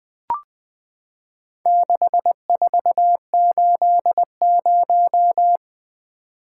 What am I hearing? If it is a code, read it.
Morse code, heard as a keyed tone, about 21 wpm, 700 Hz. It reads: 6480
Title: Four Letter-Number Combinations - 20wpm